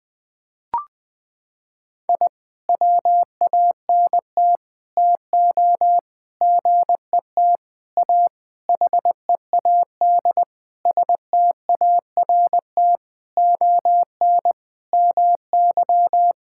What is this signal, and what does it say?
Morse code, heard as a keyed tone, about 20 wpm, 700 Hz. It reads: I WANT TO GET A HEAD START ON MY
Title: Sentences with 9-Word Phrases - 20wpm